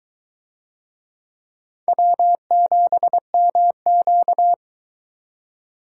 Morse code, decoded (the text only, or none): W7MQ